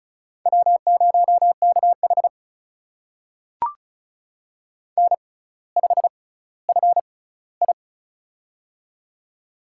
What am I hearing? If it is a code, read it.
Morse code, heard as a keyed tone, about 36 wpm, spaced as for 18 wpm, 700 Hz. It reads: W0KH N5FI
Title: USA Call Signs - 4 Characters - 6x Character Spacing - 35wpm